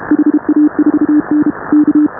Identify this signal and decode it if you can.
Morse code, heard as a keyed tone, about 29 wpm, 310 Hz. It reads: HA4N K
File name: AUDIO2019 / HA5BA CW